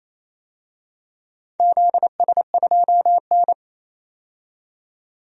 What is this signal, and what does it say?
Morse code, heard as a keyed tone, about 29 wpm, 700 Hz. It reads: ZS2N
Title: DXCC-100 Call Signs - 4 Characters - 28wpm